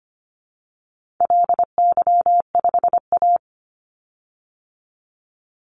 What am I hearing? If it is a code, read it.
Morse code, heard as a keyed tone, about 25 wpm, 700 Hz. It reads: LY5A
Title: Call Signs Repeated 2x - 25wpm